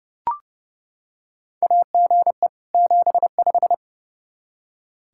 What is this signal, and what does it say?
Morse code, heard as a keyed tone, about 32 wpm, 700 Hz. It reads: AGE 75
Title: Practice QSO - Exchange 3 - More Info - 30wpm